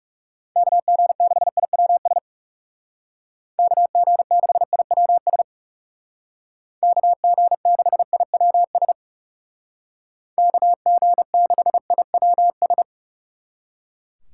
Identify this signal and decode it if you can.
Morse code, heard as a keyed tone, about 38 wpm, 700 Hz. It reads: KG6IWS KG6IWS KG6IWS KG6IWS